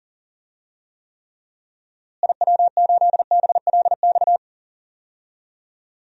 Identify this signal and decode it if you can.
Morse code, heard as a keyed tone, about 37 wpm, 700 Hz. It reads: IW8BLX
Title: DXCC-100 Call Signs - 6 Characters - Repeated 3x - 40wpm